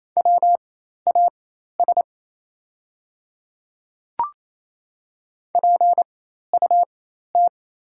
Morse code, heard as a keyed tone, about 28 wpm, spaced as for 13 wpm, 700 Hz. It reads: WAS PUT
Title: 4x Character Spacing - Top 200 Words - 28wpm